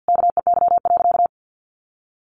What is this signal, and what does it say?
Morse code, heard as a keyed tone, about 50 wpm, 700 Hz. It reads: KEY.
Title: Sentences from Top 1400 Words 50wpm